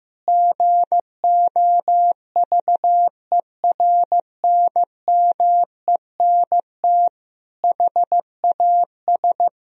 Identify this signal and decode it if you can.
Morse code, heard as a keyed tone, about 16 wpm, 700 Hz. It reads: GOVERNMENT HAS
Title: Sentences from Top 1000 Words - Review Part 3 - 15wpm